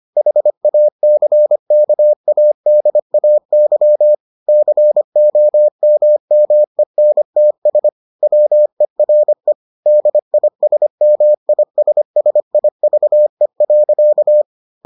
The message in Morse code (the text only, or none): HACKADAY COMMENTS WERE DISMISSIVE.